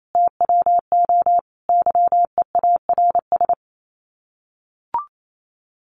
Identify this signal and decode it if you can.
Morse code, heard as a keyed tone, about 28 wpm, 700 Hz. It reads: TWO YEARS
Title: Sets of 5 Words Encore2 28wpm